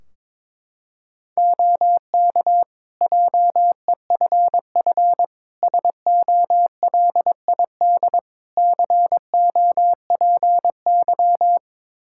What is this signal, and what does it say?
Morse code, heard as a keyed tone, about 22 wpm, 700 Hz. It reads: OK JEFF SOLID COPY
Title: K1USN SST QSO Practice 22wpm